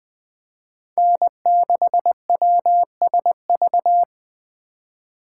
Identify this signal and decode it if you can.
Morse code, heard as a keyed tone, about 20 wpm, 700 Hz. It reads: N6WSV